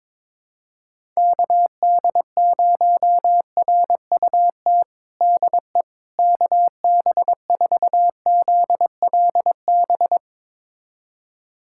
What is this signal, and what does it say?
Morse code, heard as a keyed tone, about 21 wpm, 700 Hz. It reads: KD0RUT DE KB4ZLB
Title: USA Callsign DE Callsign - 6 Characters - 22wpm